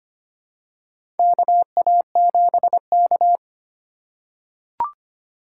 Morse code, heard as a keyed tone, about 25 wpm, 700 Hz. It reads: KA7K
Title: USA Call Signs - 4 Characters - 6x Character Spacing - 25wpm